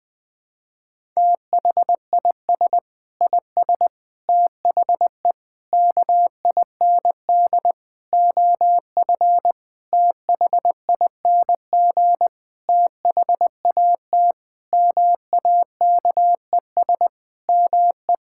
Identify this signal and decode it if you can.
Morse code, heard as a keyed tone, about 20 wpm, 700 Hz. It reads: THIS IS THE KIND OF THING THAT MAKES ME
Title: Sentences with 7-Word Phrases - 20wpm